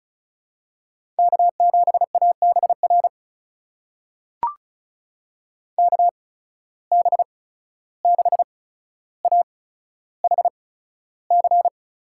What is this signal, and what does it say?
Morse code, heard as a keyed tone, about 36 wpm, spaced as for 14 wpm, 700 Hz. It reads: K7ABR KB6AHC
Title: USA Call Signs - 8x Character Spacing - 35wpm